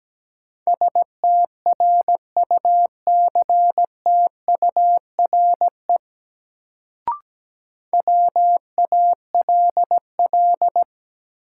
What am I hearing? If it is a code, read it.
Morse code, heard as a keyed tone, about 17 wpm, 700 Hz. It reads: STRUCTURE WALL
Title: Sentences from Top 1000 Words - Review Part 6 - New Word Drill - 17wpm